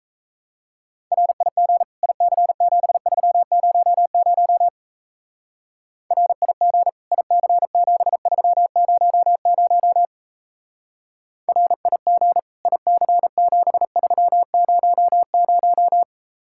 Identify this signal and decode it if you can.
Morse code, heard as a keyed tone, about 33 wpm, 700 Hz. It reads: RIG IC7300 RIG IC7300 RIG IC7300
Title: Speed-Racing Common QSO Phrases - 28wpm